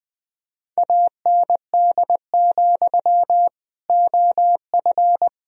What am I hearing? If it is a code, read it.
Morse code, heard as a keyed tone, about 20 wpm, 700 Hz. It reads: AND, OF
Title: Sentences from Top 1000 Words - Review Part 1 - 20wpm